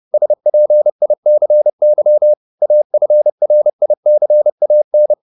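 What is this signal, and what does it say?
Morse code, heard as a keyed tone, about 30 wpm, 600 Hz. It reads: SPICY AFRICAN
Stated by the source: Morse code at 30 wpm, the novel Never by Ken Follett in English, from CW med Gnister.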